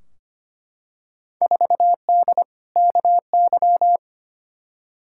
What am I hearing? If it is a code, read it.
Morse code, heard as a keyed tone, about 25 wpm, 700 Hz. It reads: 4D KY
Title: Field Day Exchanges Encore - 25wpm